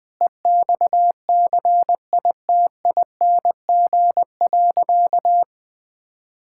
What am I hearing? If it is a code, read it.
Morse code, heard as a keyed tone, about 20 wpm, 700 Hz. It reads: EXCITING.
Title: Sentences with 5-Word Phrases - 20wpm